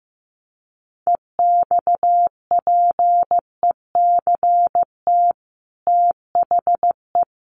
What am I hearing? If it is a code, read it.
Morse code, heard as a keyed tone, about 15 wpm, 700 Hz. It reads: EXPECT THE
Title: Sentences from Top 1100 Words 15wpm